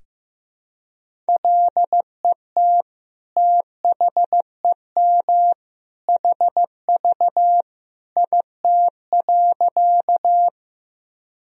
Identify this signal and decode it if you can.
Morse code, heard as a keyed tone, about 15 wpm, 700 Hz. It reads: LET THEM HV IT.
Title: Sentences from Top 300 Words and Abbreviations 15wpm